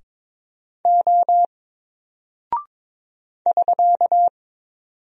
Morse code, heard as a keyed tone, about 22 wpm, 700 Hz. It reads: O <SK>
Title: Single Character (Letter:Number:Punctuation:Prosign) Alt 22wpm